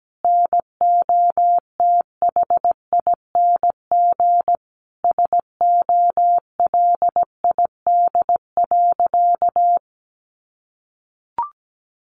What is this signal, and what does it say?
Morse code, heard as a keyed tone, about 17 wpm, 700 Hz. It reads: NOTHING SOLID.
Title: Sentences from Top 1500 Words 17wpm